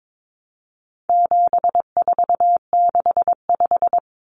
Morse code, heard as a keyed tone, about 22 wpm, 700 Hz. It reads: 7465
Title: Numbers - 4-Digit 22wpm